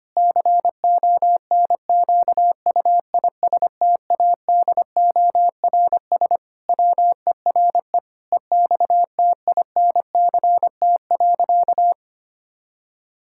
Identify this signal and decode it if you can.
Morse code, heard as a keyed tone, about 25 wpm, 700 Hz. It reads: CONQUISTADORS WERE EXTINCT.
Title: Pangram Challenge v2 - 25wpm